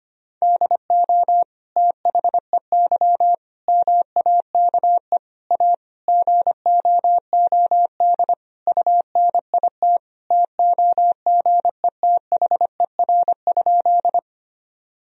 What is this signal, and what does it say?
Morse code, heard as a keyed tone, about 25 wpm, 700 Hz. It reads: DO THEY MAKE A GOOD UNIT TOGETHER?
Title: Sentences from Top 1000 Words - Review Part 2 - 25wpm